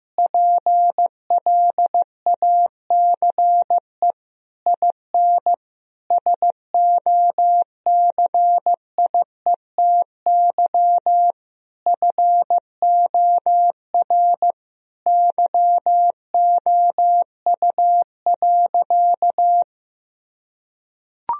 Morse code, heard as a keyed tone, about 15 wpm, 700 Hz. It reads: PLACE IN SOCIETY FOR YOU.
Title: Sentences from Top 1000 Words - Review Part 4 - 15wpm